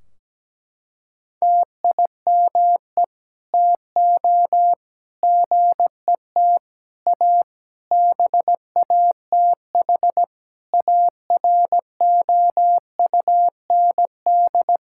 Morse code, heard as a keyed tone, about 17 wpm, 700 Hz. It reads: TIME TO GET A BATH AROUND
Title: Sentences from Top 1600 Words 17wpm